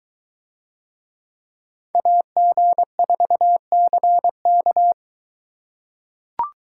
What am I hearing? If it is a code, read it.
Morse code, heard as a keyed tone, about 21 wpm, 700 Hz. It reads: AG4CK